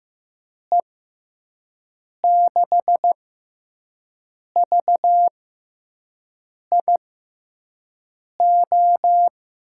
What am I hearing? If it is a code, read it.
Morse code, heard as a keyed tone, about 15 wpm, spaced as for 5 wpm, 700 Hz. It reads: E6VIO
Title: USA Call Signs - 6 Characters - 6x Character Spacing - 15wpm